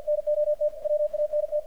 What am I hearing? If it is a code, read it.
Morse code, heard as a keyed tone, about 47 wpm, 600 Hz, with radio noise. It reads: NOT WANT
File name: archived / CRSnet